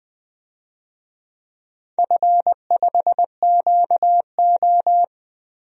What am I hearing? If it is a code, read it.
Morse code, heard as a keyed tone, about 20 wpm, 700 Hz. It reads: F5QO
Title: DXCC-100 Call Signs - 4 Characters - 20wpm